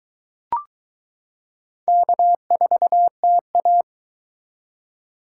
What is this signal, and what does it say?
Morse code, heard as a keyed tone, about 22 wpm, 700 Hz. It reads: K4TA